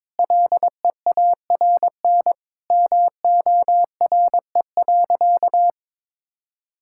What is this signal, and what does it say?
Morse code, heard as a keyed tone, about 22 wpm, 700 Hz. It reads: LEARN MORE.
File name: practice / 22wpm / Sentences with 3-Word Phrases - 22wpm